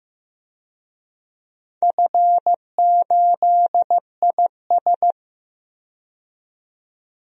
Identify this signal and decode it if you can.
Morse code, heard as a keyed tone, about 15 wpm, 700 Hz. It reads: F8IS
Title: DXCC-100 Call Signs - 4 Characters - 15wpm